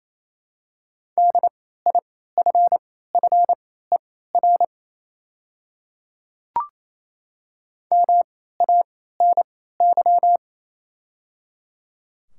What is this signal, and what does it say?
Morse code, heard as a keyed tone, about 28 wpm, spaced as for 16 wpm, 690 Hz. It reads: DIFFER MANY
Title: 3x Character Spacing - Top 300 Words - 28wpm